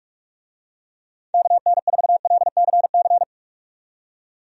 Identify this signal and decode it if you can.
Morse code, heard as a keyed tone, about 46 wpm, 700 Hz. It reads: KN4LCC
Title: USA Call Signs - 6 Characters - 45wpm